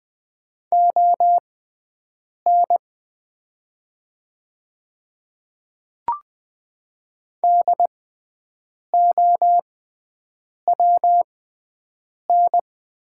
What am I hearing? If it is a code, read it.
Morse code, heard as a keyed tone, about 20 wpm, spaced as for 7 wpm, 700 Hz. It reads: ON DOWN
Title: 6x Character Spacing - Top 200 Words - 20wpm